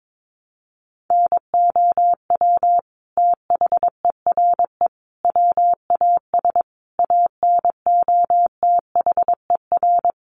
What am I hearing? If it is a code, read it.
Morse code, heard as a keyed tone, about 22 wpm, 700 Hz. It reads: NOW THERE WAS ANOTHER
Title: Sentences from Top 1300 Words 22wpm